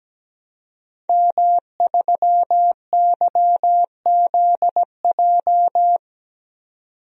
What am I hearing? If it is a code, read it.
Morse code, heard as a keyed tone, about 17 wpm, 700 Hz. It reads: M3YZJ
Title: DXCC-100 Call Signs - 5 Characters - 17wpm